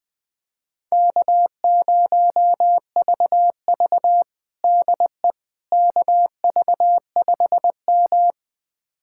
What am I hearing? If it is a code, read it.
Morse code, heard as a keyed tone, about 20 wpm, 700 Hz. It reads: K0VV DE KV5M
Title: USA Callsign DE Callsign - 4 Characters - 20wpm